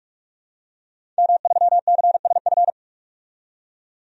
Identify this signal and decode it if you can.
Morse code, heard as a keyed tone, about 46 wpm, 700 Hz. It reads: M3CSF
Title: DXCC-100 Call Signs - 5 Characters - 45wpm